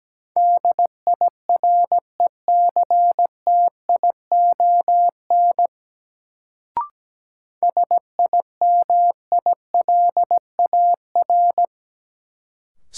Morse code, heard as a keyed tone, about 17 wpm, 700 Hz. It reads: DIRECTION SIMILAR
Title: Sentences from Top 1000 Words - Review Part 5 - New Word Drill - 17wpm